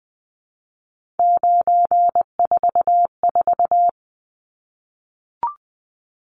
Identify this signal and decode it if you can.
Morse code, heard as a keyed tone, about 20 wpm, 700 Hz. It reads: 944
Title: Numbers - 3-Digit 20wpm